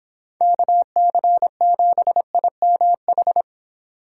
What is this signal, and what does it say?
Morse code, heard as a keyed tone, about 25 wpm, 700 Hz. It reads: KC7IMH
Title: Speed Racing - USA Call Signs 22wpm